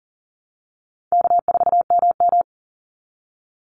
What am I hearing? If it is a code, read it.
Morse code, heard as a keyed tone, about 40 wpm, 700 Hz. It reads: K4MM
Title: Call Signs - 40wpm